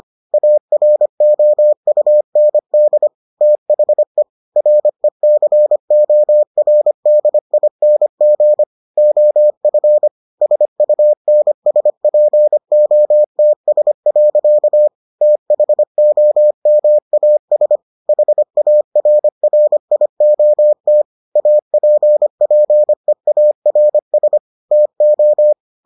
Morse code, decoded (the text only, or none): AROUND THE RECORDING OF SUNSPOTS. THOMAS HARRIOT APPEARS TO